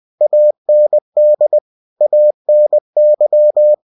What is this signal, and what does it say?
Morse code, heard as a keyed tone, about 20 wpm, 600 Hz. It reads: AND ANY
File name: cw-books / winnie-the-pooh / winnie-20wpm-page-0120